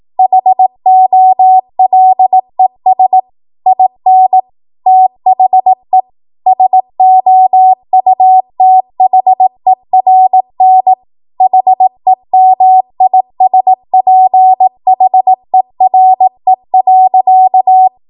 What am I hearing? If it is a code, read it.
Morse code, heard as a keyed tone, about 18 wpm, 750 Hz. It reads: HOLES IN THE SOUTHERN HEMISPHERE.
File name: Morse / 18WPM